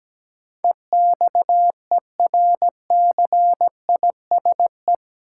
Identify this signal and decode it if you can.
Morse code, heard as a keyed tone, about 17 wpm, 700 Hz. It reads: EXERCISE
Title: Sentences from Top 1200 Words - New Word Drill Alt - 17wpm